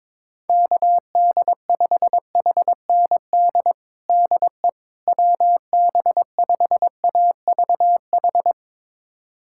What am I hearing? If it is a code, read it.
Morse code, heard as a keyed tone, about 22 wpm, 700 Hz. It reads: KD5HND DE WB5AVH
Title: USA Callsign DE Callsign - 6 Characters - 22wpm